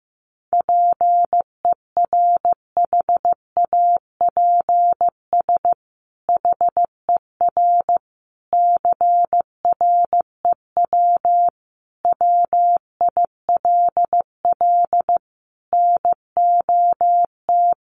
Morse code, heard as a keyed tone, about 15 wpm, 700 Hz. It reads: PERHAPS HER CREW WILL NOT
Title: Sentences from Top 1500 Words 15wpm